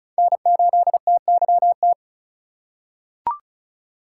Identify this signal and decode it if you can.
Morse code, heard as a keyed tone, about 36 wpm, 700 Hz. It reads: N8TYT
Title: USA Call Signs - 5 Characters - 6x Character Spacing - 35wpm